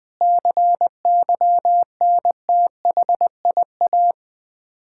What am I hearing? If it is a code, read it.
Morse code, heard as a keyed tone, about 20 wpm, 700 Hz. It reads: CYNTHIA
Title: K1USN SST QSO Practice 20wpm